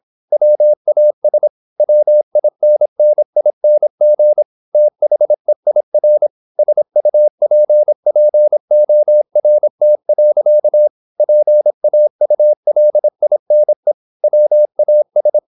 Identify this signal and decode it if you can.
Morse code, heard as a keyed tone, about 26 wpm, 600 Hz. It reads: WAS WINNING THEIR SUPPORT. PAULINE WAS
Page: Never 26wpm | CW med Gnister